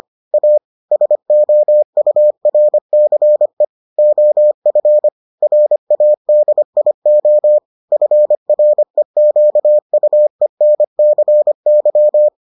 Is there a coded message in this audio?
Yes: Morse code from 0.3 to 12.3 s, heard as a keyed tone, about 25 wpm, 600 Hz. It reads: A SOURCE OF RADIO FREQUENCY